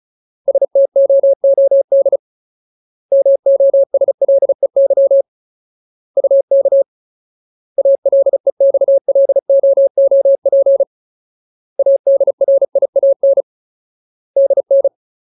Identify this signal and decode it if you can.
Morse code, heard as a keyed tone, about 37 wpm, 550 Hz. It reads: STOOD MOSLEY UK ALEXLOOP ADRIAN DN
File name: Ultimate Word List / 35WPM Ultimate